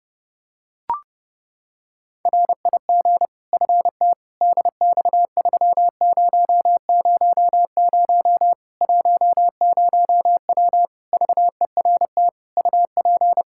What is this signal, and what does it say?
Morse code, heard as a keyed tone, about 30 wpm, 700 Hz. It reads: RIG FT DX3000 10W VERT UP
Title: Practice QSO - Exchange 2 - Info - 30wpm